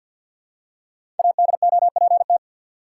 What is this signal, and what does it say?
Morse code, heard as a keyed tone, about 51 wpm, 700 Hz. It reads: ADOPT
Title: Five Letter Words - 50wpm